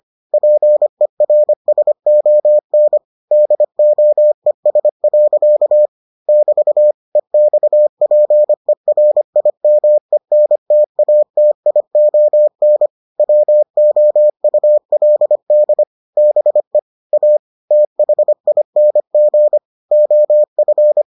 Morse code, heard as a keyed tone, about 25 wpm, 600 Hz. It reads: PERSON DOES. <BT> EXPERIMENTATION WOULD BE A THING OF